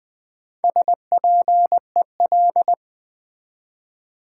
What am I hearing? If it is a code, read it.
Morse code, heard as a keyed tone, about 20 wpm, 700 Hz. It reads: SPEL
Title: Build A Word - 1 Letter At A Time - 5-Letter Words - 20wpm